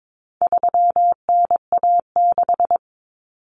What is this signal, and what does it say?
Morse code, heard as a keyed tone, about 22 wpm, 700 Hz. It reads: 3NA6
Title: Common Cut Numbers - 4-Digit 22wpm